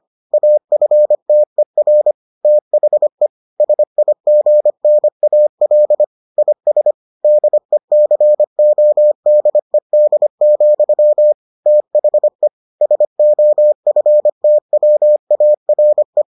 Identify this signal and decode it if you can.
Morse code, heard as a keyed tone, about 25 wpm, 600 Hz. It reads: AFTER THE SIGNAL IS DECODED, THE SOFTWARE